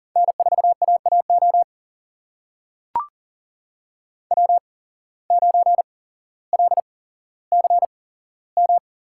Morse code, heard as a keyed tone, about 40 wpm, spaced as for 17 wpm, 700 Hz. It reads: N4AAO W9LCM
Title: USA Call Signs - 5 Characters - 8x Character Spacing - 40wpm